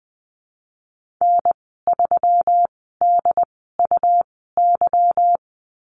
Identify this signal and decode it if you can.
Morse code, heard as a keyed tone, about 20 wpm, spaced as for 15 wpm, 700 Hz. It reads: N3DUY
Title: 207 Lesson 33g - Letter j - Call Signs 20wpm